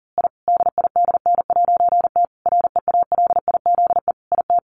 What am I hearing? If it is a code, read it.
Morse code, heard as a keyed tone, about 37 wpm, 700 Hz. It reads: I DIDN'T REALIZE IT
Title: A Sound Of Thunder 40wpm